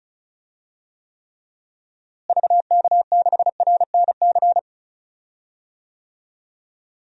Morse code, heard as a keyed tone, about 34 wpm, 700 Hz. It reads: VK6RNC
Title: DXCC-100 Call Signs - 6 Characters - Repeated 3x - 35wpm